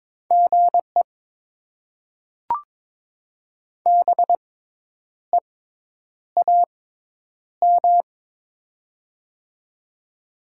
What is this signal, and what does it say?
Morse code, heard as a keyed tone, about 22 wpm, spaced as for 9 wpm, 700 Hz. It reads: GE BEAM
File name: Top 90 QSO Elements - 6x Character Spacing - 22wpm